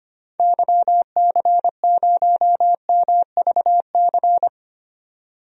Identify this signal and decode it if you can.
Morse code, heard as a keyed tone, about 25 wpm, 700 Hz. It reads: YC0MVC
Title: DXCC-100 Call Signs - 6 Characters - 25wpm